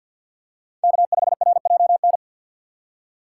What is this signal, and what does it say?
Morse code, heard as a keyed tone, about 53 wpm, 700 Hz. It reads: K5RJN
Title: USA Call Signs - 5 Characters - 6x Character Spacing - 50wpm